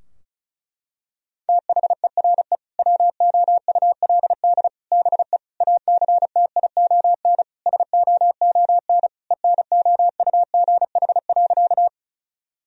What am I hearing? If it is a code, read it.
Morse code, heard as a keyed tone, about 36 wpm, 700 Hz. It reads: THERE WOULD BE ACTION SOON ENOUGH.
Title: Sentences from Top 1000 Words - Review Part 4 - 35wpm